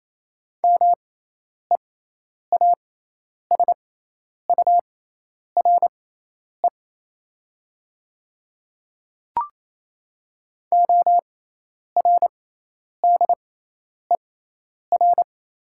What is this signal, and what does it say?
Morse code, heard as a keyed tone, about 28 wpm, spaced as for 10 wpm, 700 Hz. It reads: MEASURE ORDER
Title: 6x Character Spacing - Top 500 Words - 28wpm